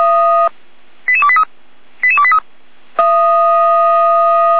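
Five-tone selective call (ZVEI), heard as quick runs of five tones, tones of about 70 ms. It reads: 88283 88283
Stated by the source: ZVEI five-tone selective call 88499 received by SDR.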